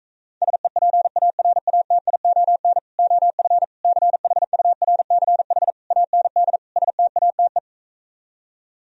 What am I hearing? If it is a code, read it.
Morse code, heard as a keyed tone, about 45 wpm, 700 Hz. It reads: SEPARATION OF CHURCH AND STATE